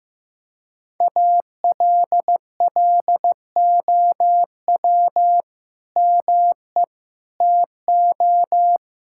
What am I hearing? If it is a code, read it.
Morse code, heard as a keyed tone, about 16 wpm, 700 Hz. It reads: ALLOW ME TO
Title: Sentences from Top 1600 Words 15wpm